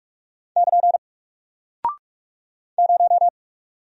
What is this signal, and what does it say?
Morse code, heard as a keyed tone, about 44 wpm, 700 Hz. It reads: <KN> 0
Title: Single Character (Letter:Number:Punctuation:Prosign) 45wpm